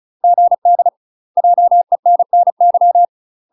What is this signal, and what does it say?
Morse code, heard as a keyed tone, about 34 wpm, 700 Hz. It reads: GD JENNY
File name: UK MASTER QSOs TRAINING / 35WPM / UK MASTER QSOs 0043